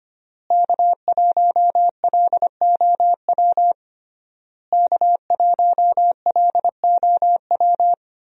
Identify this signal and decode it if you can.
Morse code, heard as a keyed tone, about 25 wpm, 700 Hz. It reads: K1LOW K1LOW
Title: USA Call Signs - 5 Characters - Repeated 3x - 25wpm